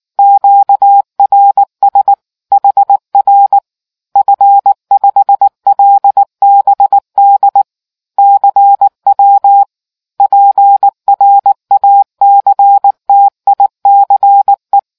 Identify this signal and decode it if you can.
Morse code, heard as a keyed tone, about 19 wpm, 800 Hz. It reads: QRS HR F5LBD CW PRACTICE